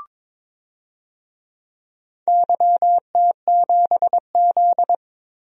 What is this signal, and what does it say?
Morse code, heard as a keyed tone, about 21 wpm, 700 Hz. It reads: YT7Z
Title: DXCC-100 Call Signs - 4 Characters - 22wpm